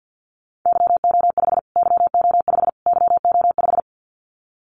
Morse code, heard as a keyed tone, about 50 wpm, 700 Hz. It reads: YO5 YO5 YO5
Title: Callsign Prefixes for Top 25 DXCC - Repeated 3x - 50wpm